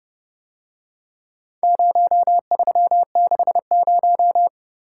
Morse code, heard as a keyed tone, about 30 wpm, 700 Hz. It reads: 0360